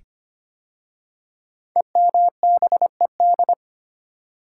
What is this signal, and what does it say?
Morse code, heard as a keyed tone, about 25 wpm, 700 Hz. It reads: EMBED
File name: 5 Letter Words - First 3 Letters Given - 25wpm